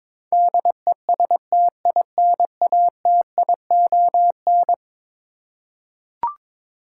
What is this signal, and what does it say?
Morse code, heard as a keyed tone, about 22 wpm, 700 Hz. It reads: DESTINATION
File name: practice / 22wpm / Eleven Letter Words - 22wpm